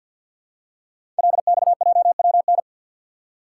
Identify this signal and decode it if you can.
Morse code, heard as a keyed tone, about 47 wpm, 700 Hz. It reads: LXJWN
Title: Code Groups - 5 Letter:Numbers - 50wpm